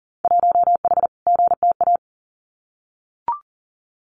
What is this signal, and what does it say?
Morse code, heard as a keyed tone, about 42 wpm, 700 Hz. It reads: 1H GTA
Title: Winter Field Day Exchanges 40wpm